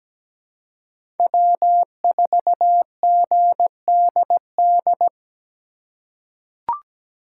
Morse code, heard as a keyed tone, about 17 wpm, 700 Hz. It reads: W4GDD
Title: USA Call Signs - 5 Characters - 3x Character Spacing - 17wpm